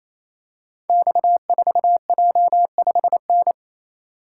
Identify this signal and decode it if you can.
Morse code, heard as a keyed tone, about 27 wpm, 700 Hz. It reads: X4J5N
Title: Code Groups - 5 Letter:Numbers - 28wpm